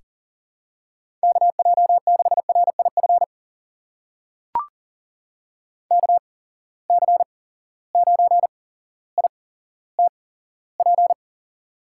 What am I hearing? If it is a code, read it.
Morse code, heard as a keyed tone, about 41 wpm, spaced as for 17 wpm, 700 Hz. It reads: KJ6RIF KC9ITP
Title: USA Call Signs - 6 Characters - 8x Character Spacing - 40wpm